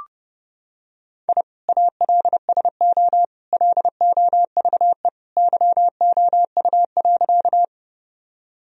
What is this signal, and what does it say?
Morse code, heard as a keyed tone, about 31 wpm, 700 Hz. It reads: I ALSO LOVE YOU.
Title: Sentences from Top 1000 Words - Review Part 1 - 30wpm